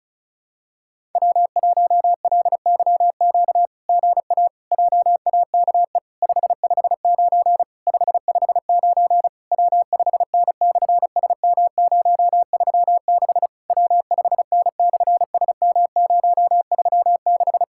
Morse code, heard as a keyed tone, about 33 wpm, 700 Hz. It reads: W1LYQ GA JAKE 559 559 W5N/SM036 W5N/SM036
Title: SOTA QSO Practice 35wpm